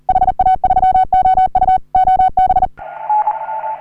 Morse code, heard as a keyed tone, about 40 wpm, 750 Hz, with radio noise. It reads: HA3OU OB
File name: CW / HA3OU RHOBCW REC